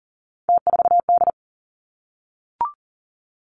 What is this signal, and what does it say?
Morse code, heard as a keyed tone, about 40 wpm, 700 Hz. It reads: T4D
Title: Three Challenging Letter-Number Combinations 40wpm